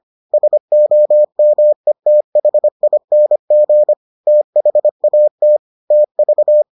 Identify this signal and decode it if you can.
Morse code, heard as a keyed tone, about 25 wpm, 600 Hz. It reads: SOMETHING THAT TV